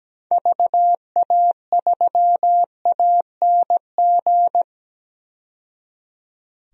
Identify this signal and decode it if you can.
Morse code, heard as a keyed tone, about 17 wpm, 700 Hz. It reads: VA3ANG
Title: DXCC-100 Call Signs - 6 Characters - 17wpm